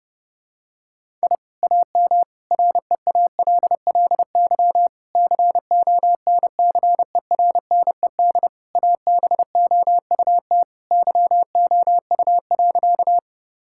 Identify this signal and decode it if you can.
Morse code, heard as a keyed tone, about 30 wpm, 700 Hz. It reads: I AM REALLY CONCERNED ABOUT YOU.
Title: Sentences from Top 1000 Words - Review Part 6 - 30wpm